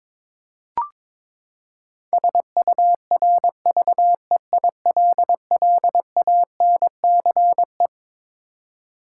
Morse code, heard as a keyed tone, about 22 wpm, 700 Hz. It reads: SURVEILLANCE
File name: Twelve Letter Words - 22wpm